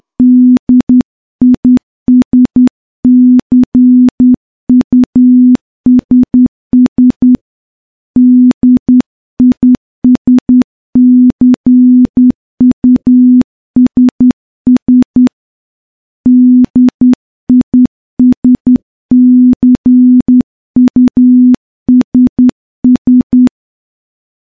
Morse code, heard as a keyed tone, about 10 wpm, 250 Hz. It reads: DISCUSS DISCUSS DISCUSS
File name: morse-training-10WPM